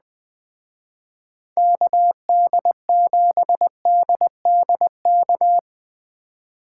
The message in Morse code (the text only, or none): KD7DDK